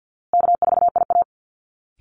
Morse code, heard as a keyed tone, about 48 wpm, 700 Hz. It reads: K4IA
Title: Easy Call Signs Repeated 2x - 50wpm